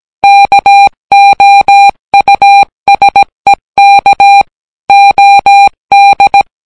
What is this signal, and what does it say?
Morse code, heard as a keyed tone, about 18 wpm, 800 Hz. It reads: KOUSEK OD